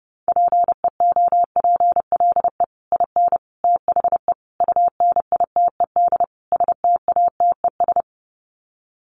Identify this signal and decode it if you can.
Morse code, heard as a keyed tone, about 30 wpm, 700 Hz. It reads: PEOPLE IN THE UNITED STATES
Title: Sets of 5 Words Encore 30wpm